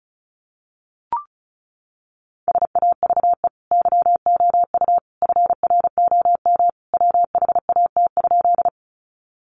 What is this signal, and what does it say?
Morse code, heard as a keyed tone, about 34 wpm, 700 Hz. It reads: SAVE YOU FROM WHAT?
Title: Sentences from Top 1100 Words 35wpm